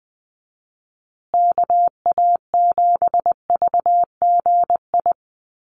Morse code, heard as a keyed tone, about 20 wpm, 700 Hz. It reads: KA7VGI
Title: USA Call Signs Repeated 2x 20wpm